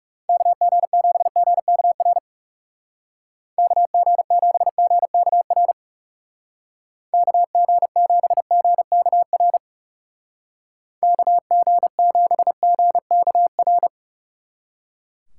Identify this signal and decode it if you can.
Morse code, heard as a keyed tone, about 36 wpm, 700 Hz. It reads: KG7GKR KG7GKR KG7GKR KG7GKR
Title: Speed Racing - USA 2x3 Call Signs 30wpm